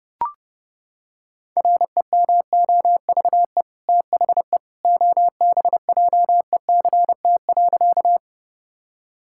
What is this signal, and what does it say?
Morse code, heard as a keyed tone, about 32 wpm, 700 Hz. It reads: REMOVE THE OBJECT.
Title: Sentences from Top 1000 Words - Review Part 5 - 30wpm